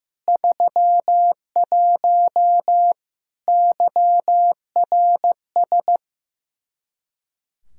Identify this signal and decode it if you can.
Morse code, heard as a keyed tone, about 15 wpm, 700 Hz. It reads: 31 YRS
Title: Practice QSO - Exchange 3 - More Info - 15wpm